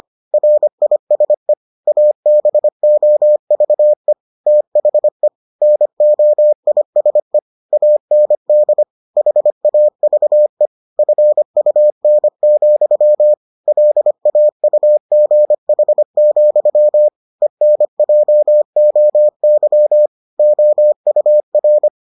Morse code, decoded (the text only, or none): RISE ABOVE THE NOISE AND HAVE FUN, LAUGH, ENJOY OUR